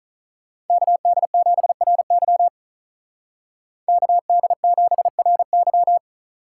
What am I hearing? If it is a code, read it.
Morse code, heard as a keyed tone, about 45 wpm, 700 Hz. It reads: KD7RY KD7RY